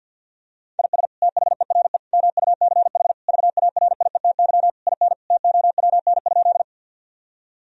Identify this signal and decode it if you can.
Morse code, heard as a keyed tone, about 50 wpm, 700 Hz. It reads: IS THERE MUCH VARIETY IN TOWN?
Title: Sentences from Top 1000 Words - Review Part 4 - 50wpm